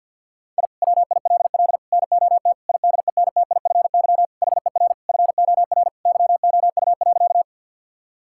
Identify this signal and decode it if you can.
Morse code, heard as a keyed tone, about 50 wpm, 700 Hz. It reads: I WILL NOT IDENTIFY HER FOR YOU.
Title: Sentences from Top 1000 Words - Review Part 6 - 50wpm